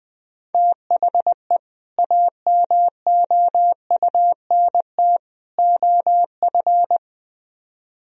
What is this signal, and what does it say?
Morse code, heard as a keyed tone, about 20 wpm, 700 Hz. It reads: THE AMOUNT OF